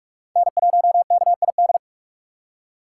Morse code, heard as a keyed tone, about 46 wpm, 700 Hz. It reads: N1KID
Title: USA Call Signs - 5 Characters - 4x Character Spacing - 45wpm